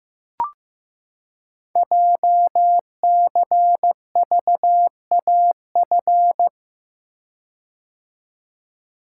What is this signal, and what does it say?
Morse code, heard as a keyed tone, about 15 wpm, 700 Hz. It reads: JCVAF
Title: Code Groups - 5 Letters - 15wpm